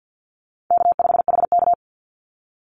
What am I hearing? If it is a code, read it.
Morse code, heard as a keyed tone, about 50 wpm, 700 Hz. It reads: K5HK
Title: Easy Call Signs - 50wpm